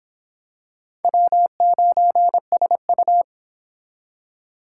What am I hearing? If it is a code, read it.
Morse code, heard as a keyed tone, about 26 wpm, 700 Hz. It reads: W9SU